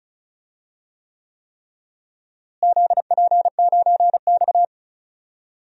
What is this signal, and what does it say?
Morse code, heard as a keyed tone, about 36 wpm, 700 Hz. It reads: ZP9X
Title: DXCC-100 Call Signs - 4 Characters - Repeated 3x - 35wpm